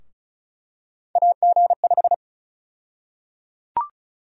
Morse code, heard as a keyed tone, about 34 wpm, 700 Hz. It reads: AG5
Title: USA Call Sign Prefixes - Repeated 2x - 35wpm